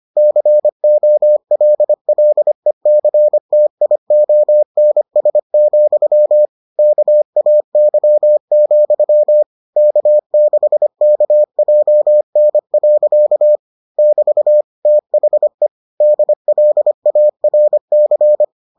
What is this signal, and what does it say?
Morse code, heard as a keyed tone, about 25 wpm, 600 Hz. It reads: COLLECTIONS, KAY, K6KJN. <BT> THE DLARC